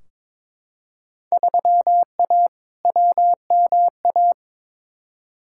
Morse code, heard as a keyed tone, about 21 wpm, 700 Hz. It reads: 3A WMA
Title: Field Day Exchanges Encore - 22wpm